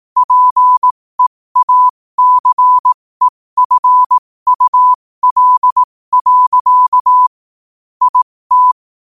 Morse code, heard as a keyed tone, about 18 wpm, spaced as for 15 wpm, 1000 Hz. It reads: PEACEFUL. IT
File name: qotdfiles / 15wpm